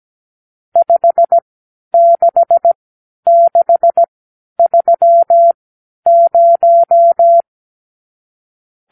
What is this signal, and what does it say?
Morse code, heard as a keyed tone, about 17 wpm, spaced as for 10 wpm, 680 Hz. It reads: 56630
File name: arrl / Review (1-0)